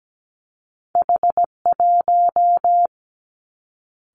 Morse code, heard as a keyed tone, about 17 wpm, 700 Hz. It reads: H1
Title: Two Challenging Letter-Number Combinations 17wpm